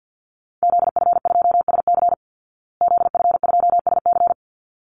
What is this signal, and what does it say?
Morse code, heard as a keyed tone, about 50 wpm, 700 Hz. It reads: ZF2SC ZF2SC
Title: Call Signs Repeated 3x - 50wpm